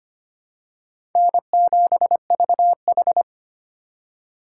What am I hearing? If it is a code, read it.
Morse code, heard as a keyed tone, about 25 wpm, 700 Hz. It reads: N7VH
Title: USA Call Signs - 3x Character Spacing - 25wpm